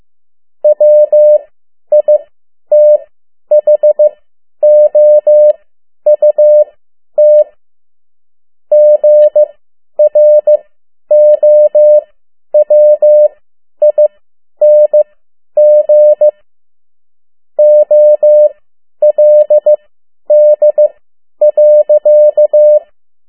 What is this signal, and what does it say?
Morse code, heard as a keyed tone, about 15 wpm, spaced as for 10 wpm, 600 Hz. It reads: WITHOUT GROWING OLD.
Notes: Morse code 10 WPM